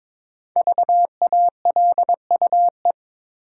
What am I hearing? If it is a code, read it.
Morse code, heard as a keyed tone, about 22 wpm, 700 Hz. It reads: VALUE
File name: 5 Letter Words - First 3 Letters Given - 22wpm